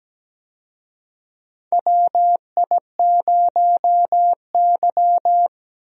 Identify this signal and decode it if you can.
Morse code, heard as a keyed tone, about 17 wpm, 700 Hz. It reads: WI0Y
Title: USA Call Signs - 4 Characters - Repeated 2x - 17wpm